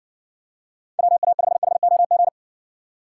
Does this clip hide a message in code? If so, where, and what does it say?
Yes: Morse code from 1.0 to 2.3 s, heard as a keyed tone, about 65 wpm, 700 Hz. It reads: WA5HOG